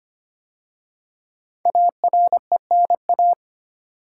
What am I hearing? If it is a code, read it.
Morse code, heard as a keyed tone, about 25 wpm, 700 Hz. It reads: ARENA
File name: 5 Letter Words - First 3 Letters Given - 25wpm